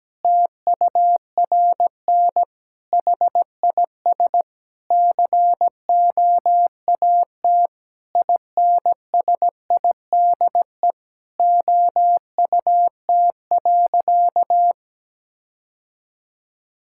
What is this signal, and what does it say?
Morse code, heard as a keyed tone, about 17 wpm, 700 Hz. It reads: TURN HIS COAT INSIDE OUT.
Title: Sentences from Top 1600 Words 17wpm